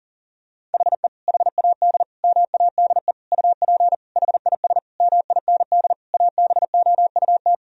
Morse code, heard as a keyed tone, about 40 wpm, 700 Hz. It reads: HE HAD MADE UP HIS MIND ABOUT
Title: Sentences from Top 1000 Words - Review Part 3 - 40wpm